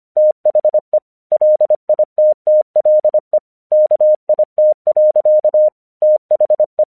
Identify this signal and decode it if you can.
Morse code, heard as a keyed tone, about 25 wpm, 610 Hz. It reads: THE LITTLE KIT. THE
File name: 1 English 25wpm / AQC26